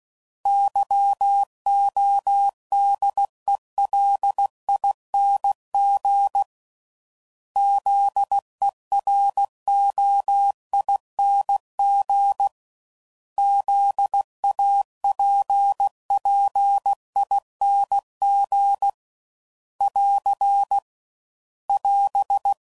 Morse code, heard as a keyed tone, about 16 wpm, 800 Hz. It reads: YODELING ZEROING ZAPPING <AR> <AS>
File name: Suffixes / Special INGs / Sngl Ings